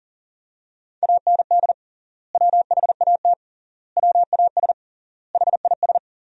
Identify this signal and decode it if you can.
Morse code, heard as a keyed tone, about 40 wpm, 700 Hz. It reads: AND WHAT WAS HIS
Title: 3x Word-Spacing - Sentences from Top 200 Words - 40wpm